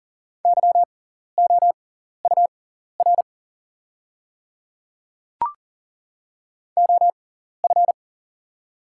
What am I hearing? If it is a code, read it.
Morse code, heard as a keyed tone, about 40 wpm, spaced as for 14 wpm, 690 Hz. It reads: YOUR OF
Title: 6x Character Spacing - Top 100 Words - 40wpm